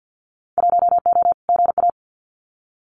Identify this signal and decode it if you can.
Morse code, heard as a keyed tone, about 50 wpm, 700 Hz. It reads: 1O GA
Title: Winter Field Day Exchanges 50wpm